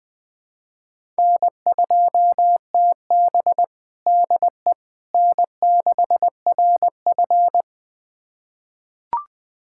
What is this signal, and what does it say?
Morse code, heard as a keyed tone, about 20 wpm, 700 Hz. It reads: N2TB DE N6RF
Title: USA Callsign DE Callsign - 4 Characters - 20wpm